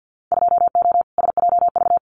Morse code, heard as a keyed tone, about 50 wpm, 700 Hz. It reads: 2O SJV
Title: Winter Field Day Exchanges Encore 50wpm